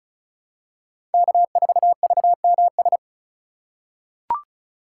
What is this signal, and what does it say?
Morse code, heard as a keyed tone, about 34 wpm, 700 Hz. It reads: K4VMS
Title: USA Call Signs - 5 Characters - 6x Character Spacing - 35wpm